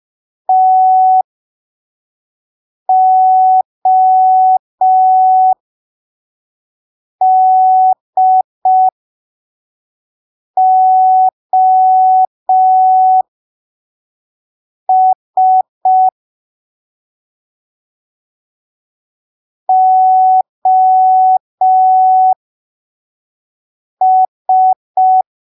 Morse code, heard as a keyed tone, about 5 wpm, spaced as for 3 wpm, 750 Hz. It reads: TODOS OS
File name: arquivos / texto / texto 25 05ppm